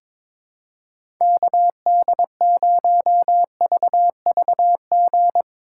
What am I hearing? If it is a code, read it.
Morse code, heard as a keyed tone, about 22 wpm, 700 Hz. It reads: KD0VVG
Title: USA Call Signs - 4x Character Spacing - 22wpm